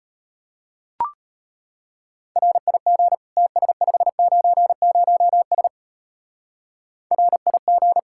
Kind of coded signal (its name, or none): Morse code